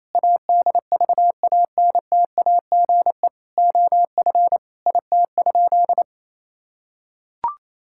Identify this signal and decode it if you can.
Morse code, heard as a keyed tone, about 27 wpm, 700 Hz. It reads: ADVANTAGE OF IT?
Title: Sentences from Top 1000 Words - Review Part 6 - 28wpm